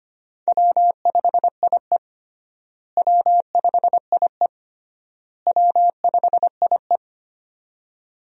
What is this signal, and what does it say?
Morse code, heard as a keyed tone, about 25 wpm, 700 Hz. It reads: W5IE W5IE W5IE
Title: USA Call Signs - 4 Characters - Repeated 3x - 25wpm